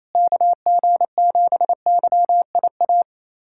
Morse code, heard as a keyed tone, about 28 wpm, 700 Hz. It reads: KG7YIA
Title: USA Callsign DE Callsign - 6 Characters - 28wpm